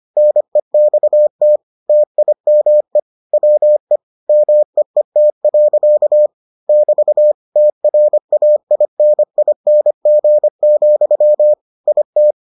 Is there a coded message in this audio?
Yes: Morse code from 0.2 to 12.3 s, heard as a keyed tone, about 25 wpm, 600 Hz. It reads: NEXT TIME WE MEET. <BT> TRAINING, IT